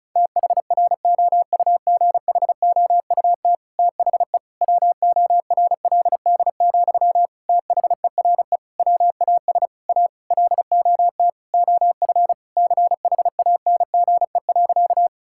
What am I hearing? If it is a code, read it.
Morse code, heard as a keyed tone, about 34 wpm, 700 Hz. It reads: THROUGHOUT THE WORLD, THERE WAS A LOT OF CHANGE.
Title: Sentences from Top 1000 Words - Review Part 5 - 35wpm